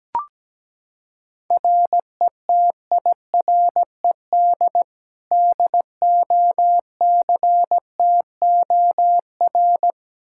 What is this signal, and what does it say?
Morse code, heard as a keyed tone, about 17 wpm, 700 Hz. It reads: RETIRED DOCTOR
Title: Practice QSO - Exchange 3 - More Info - 17wpm